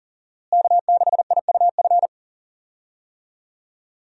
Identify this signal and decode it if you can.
Morse code, heard as a keyed tone, about 40 wpm, 700 Hz. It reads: K6IUF
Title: USA Call Signs - 5 Characters - Repeated 2x - 40wpm